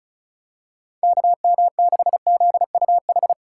Morse code, heard as a keyed tone, about 36 wpm, 700 Hz. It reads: KM6ZUH
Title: USA Call Signs - 6 Characters - 2x Character Spacing - 35wpm